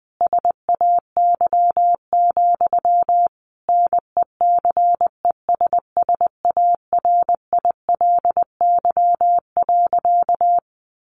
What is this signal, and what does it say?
Morse code, heard as a keyed tone, about 20 wpm, 700 Hz. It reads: SAY, NECESSARILY.
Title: Sentences from Top 1400 Words 20wpm